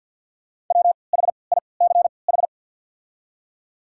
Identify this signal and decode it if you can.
Morse code, heard as a keyed tone, about 53 wpm, spaced as for 29 wpm, 690 Hz. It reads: WHICH
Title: 3x Character Spacing - Top 100 Words - 50wpm